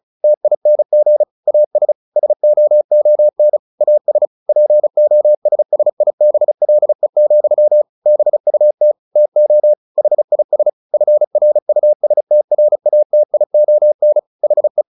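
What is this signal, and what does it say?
Morse code, heard as a keyed tone, about 34 wpm, 600 Hz. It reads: TING AS SOON AS POSSIBLE, BUT TO HIS FRUSTRATION HE
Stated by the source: Morse practice at 35 wpm from CW med Gnister, sending Ken Follett's novel Never in English.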